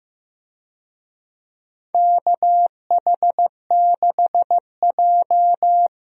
Morse code, heard as a keyed tone, about 15 wpm, 700 Hz. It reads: KH6J
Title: USA Call Signs - 4 Characters - Repeated 3x - 15wpm